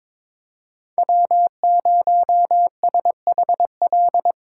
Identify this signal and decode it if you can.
Morse code, heard as a keyed tone, about 22 wpm, 700 Hz. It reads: W0SHL
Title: K1USN SST QSO Practice - 22wpm